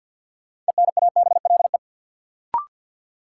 Morse code, heard as a keyed tone, about 53 wpm, 700 Hz. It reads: ENABLE
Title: Sentences from Top 1500 Words - New Word Drill Alt - 50wpm